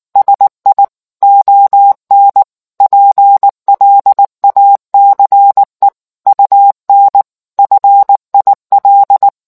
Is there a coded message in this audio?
Yes: Morse code from 0.2 to 9.3 s, heard as a keyed tone, about 19 wpm, 800 Hz. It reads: SI ON PLACE UN FIL